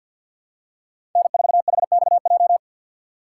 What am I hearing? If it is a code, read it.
Morse code, heard as a keyed tone, about 52 wpm, 700 Hz. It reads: N4HXJ
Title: USA Call Signs - 2x Character Spacing - 50wpm